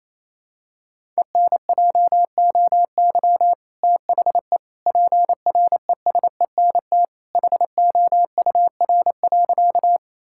Morse code, heard as a keyed tone, about 27 wpm, 700 Hz. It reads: ENJOY THE PRESENT HOUR.
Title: Sentences from Top 1000 Words - Review Part 5 - 28wpm